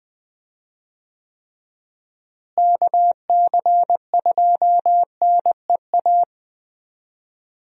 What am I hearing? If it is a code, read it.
Morse code, heard as a keyed tone, about 20 wpm, 700 Hz. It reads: KC2NEA
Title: USA Call Signs - 6 Characters - Repeated 2x - 20wpm